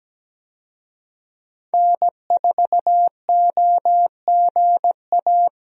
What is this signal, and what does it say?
Morse code, heard as a keyed tone, about 17 wpm, 700 Hz. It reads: N4OGA
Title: USA Call Signs - 5 Characters - 17wpm